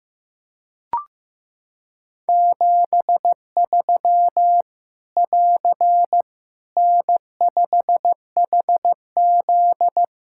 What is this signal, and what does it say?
Morse code, heard as a keyed tone, about 15 wpm, 700 Hz. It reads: 73 <AR> N5HZ
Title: Practice QSO - Exchange 4 - Ending - 15wpm